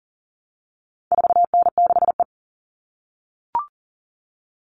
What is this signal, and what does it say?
Morse code, heard as a keyed tone, about 40 wpm, 700 Hz. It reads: 4N6E
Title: All Cut Numbers - 4-Digit 40wpm